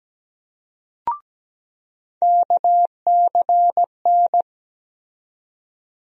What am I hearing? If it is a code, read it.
Morse code, heard as a keyed tone, about 17 wpm, 700 Hz. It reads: KCN
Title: USA Call Sign Postfixes - 3 Letters 17wpm